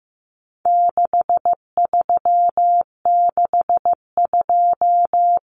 Morse code, heard as a keyed tone, about 15 wpm, 700 Hz. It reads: 6362
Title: Numbers - 4-Digit 15wpm